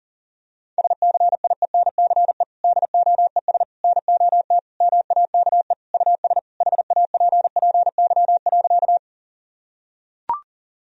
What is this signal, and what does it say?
Morse code, heard as a keyed tone, about 37 wpm, 700 Hz. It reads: SCIENCE DOES NOT MAKE US HAPPY.
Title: Sentences from Top 1000 Words - Review Part 2 - 40wpm